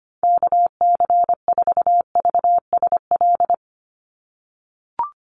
Morse code, heard as a keyed tone, about 25 wpm, 700 Hz. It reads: KC4VSL
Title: USA Call Signs 25wpm